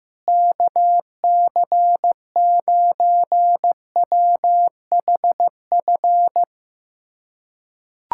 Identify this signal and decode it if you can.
Morse code, heard as a keyed tone, about 16 wpm, 700 Hz. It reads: KC9WHF
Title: USA Callsign DE Callsign - 6 Characters - 15wpm